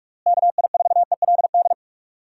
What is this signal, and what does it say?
Morse code, heard as a keyed tone, about 42 wpm, 700 Hz. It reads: KI4ELD
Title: USA Call Signs - 6 Characters - Repeated 3x - 45wpm